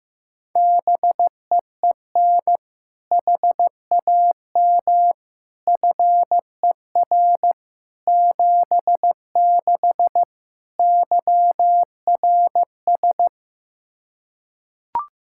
Morse code, heard as a keyed tone, about 16 wpm, 700 Hz. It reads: BEEN HAM FER 76 YRS
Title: Practice QSO - Exchange 3 - More Info - 15wpm